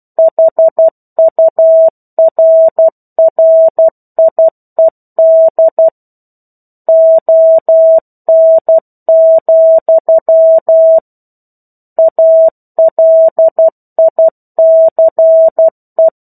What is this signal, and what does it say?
Morse code, heard as a keyed tone, about 12 wpm, 650 Hz. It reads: HURRIED ON, ALICE
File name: 01-Chapter-01 12 WPM